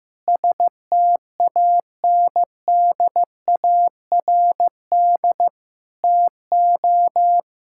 Morse code, heard as a keyed tone, about 15 wpm, 700 Hz. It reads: STANDARD TO
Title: Sentences from Top 1000 Words - Review Part 6 - 15wpm